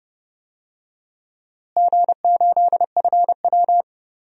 Morse code, heard as a keyed tone, about 31 wpm, 700 Hz. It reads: G8FW
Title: DXCC-100 Call Signs - 4 Characters - Repeated 3x - 30wpm